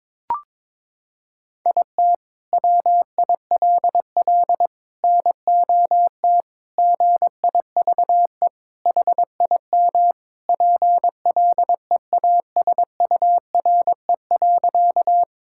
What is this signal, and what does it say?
Morse code, heard as a keyed tone, about 22 wpm, 700 Hz. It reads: IT WILL NOT GIVE HIM PLEASURE.
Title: Sentences from Top 1600 Words 22wpm